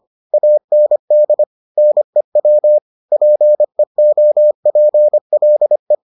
Morse code, heard as a keyed tone, about 25 wpm, 600 Hz. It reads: AND NEW PEOPLE